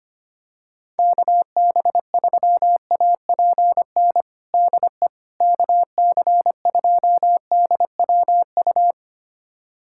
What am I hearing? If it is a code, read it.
Morse code, heard as a keyed tone, about 25 wpm, 700 Hz. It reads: KB3APN DE KC2DWU
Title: USA Callsign DE Callsign - 6 Characters - 25wpm